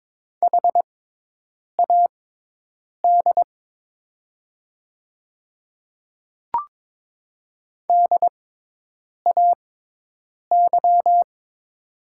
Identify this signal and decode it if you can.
Morse code, heard as a keyed tone, about 22 wpm, spaced as for 8 wpm, 700 Hz. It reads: HAD DAY
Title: 6x Character Spacing - Top 200 Words - 22wpm